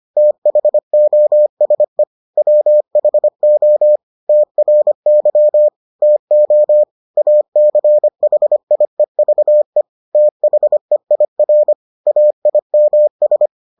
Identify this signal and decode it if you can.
Morse code, heard as a keyed tone, about 25 wpm, 600 Hz. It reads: THOSE WHO TRY TO ACHIEVE THEIR AIMS